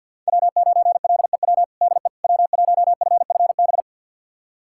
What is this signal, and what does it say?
Morse code, heard as a keyed tone, about 51 wpm, 700 Hz. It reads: W9LEW DE W1FFB
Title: USA Callsign DE Callsign - 5 Characters - 50wpm